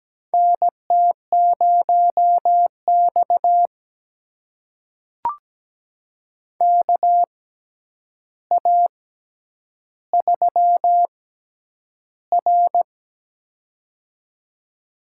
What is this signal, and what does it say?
Morse code, heard as a keyed tone, about 17 wpm, spaced as for 9 wpm, 700 Hz. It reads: NT0X KA3R
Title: USA Call Signs - 4 Characters - 6x Character Spacing - 17wpm